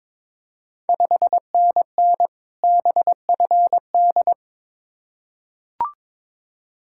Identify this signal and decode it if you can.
Morse code, heard as a keyed tone, about 22 wpm, 690 Hz. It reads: 5NN BFD
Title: Ontario QSO Party Exchange 22wpm